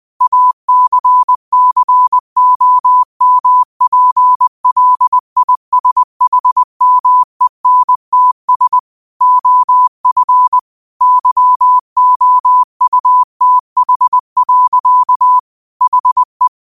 Morse code, heard as a keyed tone, about 20 wpm, 1000 Hz. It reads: ACCOMPLISHMENTS OF YOUTH. HE